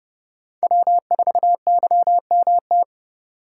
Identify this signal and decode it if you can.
Morse code, heard as a keyed tone, about 29 wpm, 700 Hz. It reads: W4YMT